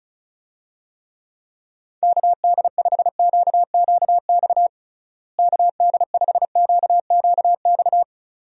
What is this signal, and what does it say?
Morse code, heard as a keyed tone, about 36 wpm, 700 Hz. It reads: KD5QQX KD5QQX
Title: USA Call Signs - 6 Characters - Repeated 2x - 35wpm